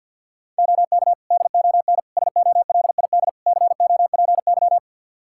Morse code, heard as a keyed tone, about 51 wpm, 700 Hz. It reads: OK DON SOLID COPY
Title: Practice QSO - Exchange 3 - More Info - 50wpm